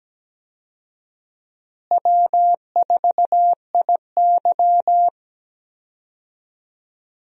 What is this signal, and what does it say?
Morse code, heard as a keyed tone, about 17 wpm, 700 Hz. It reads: W4IY
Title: USA Call Signs - 4 Characters - Repeated 3x - 17wpm